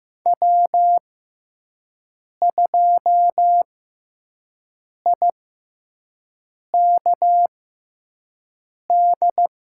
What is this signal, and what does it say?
Morse code, heard as a keyed tone, about 15 wpm, spaced as for 5 wpm, 700 Hz. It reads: W2IKD
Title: USA Call Signs - 5 Characters - 6x Character Spacing - 15wpm